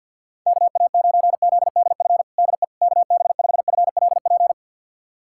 Morse code, heard as a keyed tone, about 52 wpm, 700 Hz. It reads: KA9ZDF DE KB5FLP
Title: USA Callsign DE Callsign - 6 Characters - 50wpm